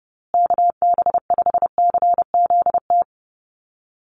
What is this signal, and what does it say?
Morse code, heard as a keyed tone, about 30 wpm, 700 Hz. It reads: KB5CZT
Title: USA Call Signs Repeated 2x 30wpm